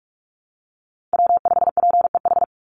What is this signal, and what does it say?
Morse code, heard as a keyed tone, about 44 wpm, 700 Hz. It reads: W5PEH
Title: Call Signs Repeated 3x - 45wpm